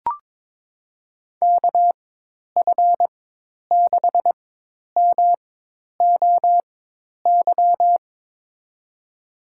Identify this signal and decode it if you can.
Morse code, heard as a keyed tone, about 22 wpm, spaced as for 10 wpm, 700 Hz. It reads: KF6MOY
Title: USA Call Signs - 6 Characters - 4x Character Spacing - 22wpm